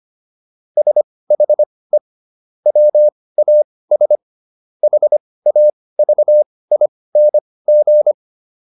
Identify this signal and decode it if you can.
Morse code, heard as a keyed tone, about 25 wpm, spaced as for 18 wpm, 600 Hz. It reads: SHE WAS HAVING